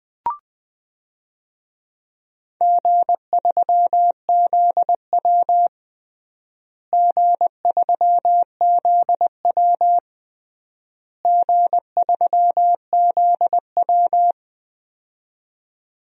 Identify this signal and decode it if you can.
Morse code, heard as a keyed tone, about 20 wpm, 700 Hz. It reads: G3ZW G3ZW G3ZW
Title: DXCC-100 Call Signs - 4 Characters - Repeated 3x - 20wpm